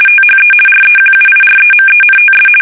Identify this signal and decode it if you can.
RTTY, about 50 baud. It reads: THE REQUIREMENTS
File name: fsk850-50B / teletype-corp